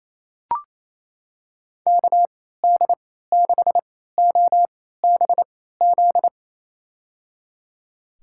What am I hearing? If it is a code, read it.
Morse code, heard as a keyed tone, about 28 wpm, spaced as for 16 wpm, 700 Hz. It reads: KD6OBZ
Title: USA Call Signs - 6 Characters - 3x Character Spacing - 28wpm